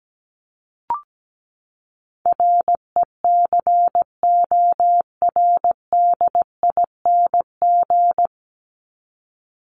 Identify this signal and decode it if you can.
Morse code, heard as a keyed tone, about 17 wpm, 700 Hz. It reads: RECORDING
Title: Top 1501-1750 Words 17wpm